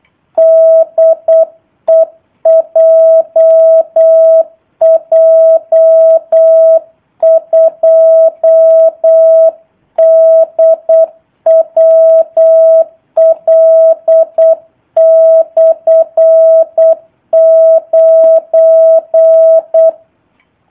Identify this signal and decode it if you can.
Morse code, heard as a keyed tone, about 8 wpm, 650 Hz. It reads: DE JJ2DWL/9
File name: jh8cbh / 2019cw / sp03- w